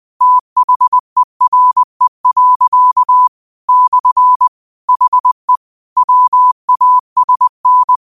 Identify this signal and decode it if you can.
Morse code, heard as a keyed tone, about 20 wpm, 1000 Hz. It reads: THERE. / HE WASN